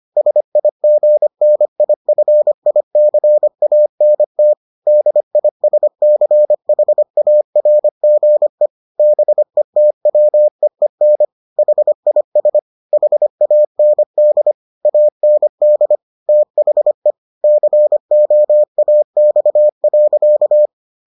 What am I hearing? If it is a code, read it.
Morse code, heard as a keyed tone, about 25 wpm, 600 Hz. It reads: SIGNIFICANT DISCHARGE BETWEEN HIS HAND AND THE COAX.